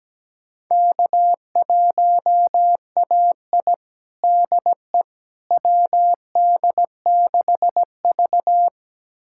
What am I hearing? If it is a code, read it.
Morse code, heard as a keyed tone, about 17 wpm, 700 Hz. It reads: K1AI DE WD6V
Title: USA Callsign DE Callsign - 4 Characters - 17wpm